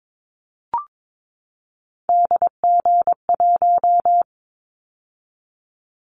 Morse code, heard as a keyed tone, about 22 wpm, 700 Hz. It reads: DG1
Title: Callsign Prefixes for Top 50 DXCC - 22wpm